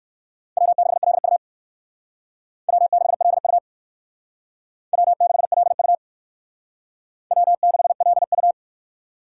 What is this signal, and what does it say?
Morse code, heard as a keyed tone, about 58 wpm, 700 Hz. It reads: W6LU W6LU W6LU W6LU